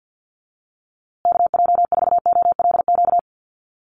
Morse code, heard as a keyed tone, about 52 wpm, 700 Hz. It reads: KJ4OLQ
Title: USA Call Signs Repeated 2x 50wpm